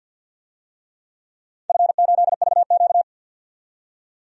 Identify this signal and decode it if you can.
Morse code, heard as a keyed tone, about 51 wpm, 700 Hz. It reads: F8VQ
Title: DXCC-100 Call Signs - 4 Characters - 50wpm